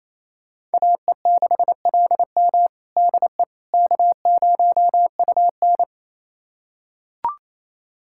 Morse code, heard as a keyed tone, about 28 wpm, 700 Hz. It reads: AE6LM DE K0UN
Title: USA Callsign DE Callsign 28wpm